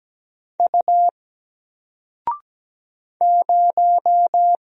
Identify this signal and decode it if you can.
Morse code, heard as a keyed tone, about 17 wpm, 700 Hz. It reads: U 0
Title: Single Letter-Number Alt 17wpm